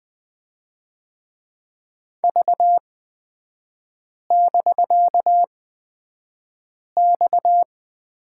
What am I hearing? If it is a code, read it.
Morse code, heard as a keyed tone, about 20 wpm, spaced as for 5 wpm, 700 Hz. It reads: V<BK>X
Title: 250 Lesson 40e - BK - Three Characters 20wpm